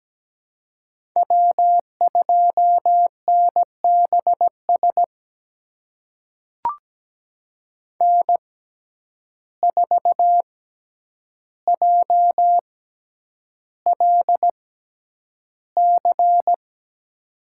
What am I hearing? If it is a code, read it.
Morse code, heard as a keyed tone, about 17 wpm, spaced as for 9 wpm, 700 Hz. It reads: W2NBS N4JLC
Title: USA Call Signs - 6x Character Spacing - 17wpm